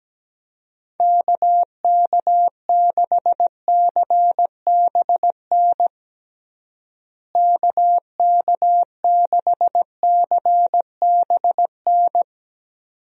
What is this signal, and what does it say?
Morse code, heard as a keyed tone, about 17 wpm, 700 Hz. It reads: KK6CBN KK6CBN
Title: USA Call Signs - 6 Characters - Repeated 2x - 17wpm